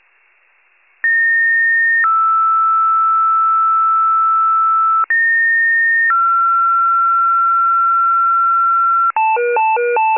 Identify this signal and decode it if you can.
Two-tone page, heard as a one-second tone, then a three-second tone, about 1850 then 1320 Hz, about 1850 then 1360 Hz. It is heard from 1.0 to 9.1 s.
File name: pager / 21 FIRE / wcalert